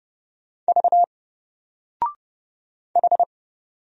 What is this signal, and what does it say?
Morse code, heard as a keyed tone, about 31 wpm, 700 Hz. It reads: V H
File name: Single-Letter-Number-30wpm